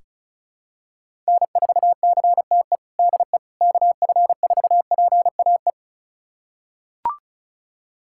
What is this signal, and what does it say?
Morse code, heard as a keyed tone, about 36 wpm, 700 Hz. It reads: N4CTE DE KF4PAE
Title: USA Callsign DE Callsign 35wpm